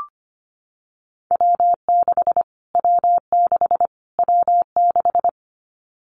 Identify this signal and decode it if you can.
Morse code, heard as a keyed tone, about 25 wpm, 700 Hz. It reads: W6 W6 W6
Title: Callsign Prefixes for Top 25 DXCC - Repeated 3x - 25wpm